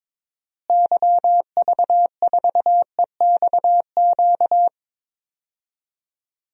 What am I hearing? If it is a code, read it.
Morse code, heard as a keyed tone, about 21 wpm, 700 Hz. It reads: YV4EXQ
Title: DXCC-100 Call Signs - 6 Characters - Repeated 2x - 22wpm